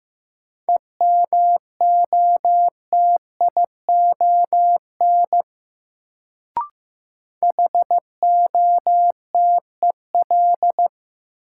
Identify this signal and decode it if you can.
Morse code, heard as a keyed tone, about 15 wpm, 700 Hz. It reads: EMOTION HOTEL
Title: Sentences from Top 1600 Words - New Word Drill - 15wpm